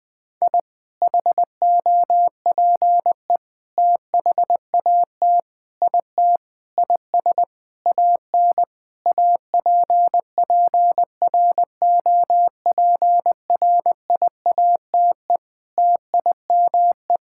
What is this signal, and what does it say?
Morse code, heard as a keyed tone, about 20 wpm, 700 Hz. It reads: I HOPE THAT IT IS AN APPROPRIATE TIME
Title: Sentences from Top 1000 Words - Review Part 5 - 20wpm